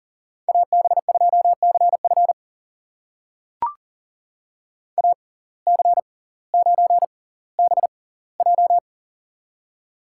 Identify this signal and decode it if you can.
Morse code, heard as a keyed tone, about 40 wpm, spaced as for 20 wpm, 700 Hz. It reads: AB2CF AC9BJ
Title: USA Call Signs - 5 Characters - 6x Character Spacing - 40wpm